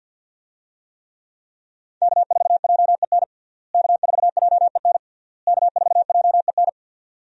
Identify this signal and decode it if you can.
Morse code, heard as a keyed tone, about 46 wpm, 700 Hz. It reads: K4JEN K4JEN K4JEN
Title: USA Call Signs - 5 Characters - Repeated 3x - 50wpm